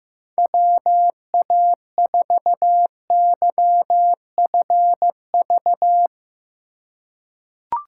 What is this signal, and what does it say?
Morse code, heard as a keyed tone, about 15 wpm, 700 Hz. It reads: WA4YFV
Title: USA Call Signs - 6 Characters - 2x Character Spacing - 15wpm